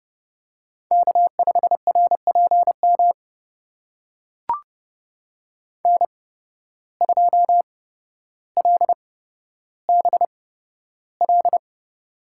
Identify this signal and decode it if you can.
Morse code, heard as a keyed tone, about 30 wpm, spaced as for 13 wpm, 700 Hz. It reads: K5RPM N2LBL
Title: USA Call Signs - 5 Characters - 8x Character Spacing - 30wpm